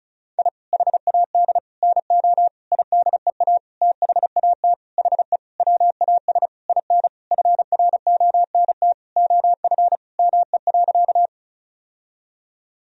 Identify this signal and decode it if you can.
Morse code, heard as a keyed tone, about 34 wpm, 700 Hz. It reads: I HAD NO IDEA THAT HE WAS IN FRONT OF ME.
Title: Sentences from Top 1000 Words - Review Part 2 - 35wpm